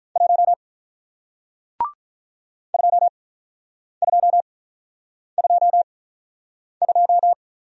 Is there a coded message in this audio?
Yes: Morse code from 0.2 to 7.3 s, heard as a keyed tone, about 48 wpm, 700 Hz. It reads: 1 2 2 2 2